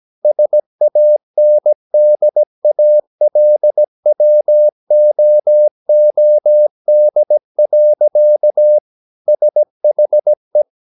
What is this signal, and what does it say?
Morse code, heard as a keyed tone, about 17 wpm, 600 Hz. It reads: SANDALWOOD. SHE
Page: Romanen Never af Ken Follet på engelsk.